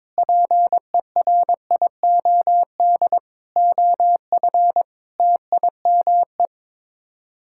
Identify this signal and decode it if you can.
Morse code, heard as a keyed tone, about 21 wpm, 700 Hz. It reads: PERIOD OF TIME
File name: practice / 22wpm / Sets of 6 Words - 22wpm